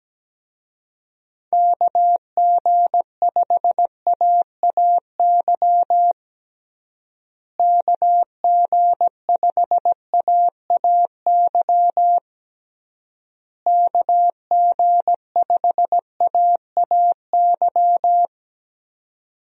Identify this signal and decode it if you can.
Morse code, heard as a keyed tone, about 17 wpm, 700 Hz. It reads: KG5AAY KG5AAY KG5AAY
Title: USA Call Signs - 6 Characters - Repeated 3x - 17wpm